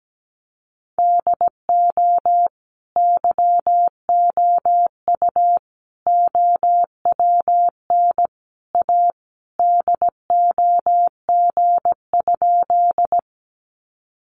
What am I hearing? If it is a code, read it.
Morse code, heard as a keyed tone, about 17 wpm, 700 Hz. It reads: DO YOU OWN A DOG?
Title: Sentences from Top 400 Words 17wpm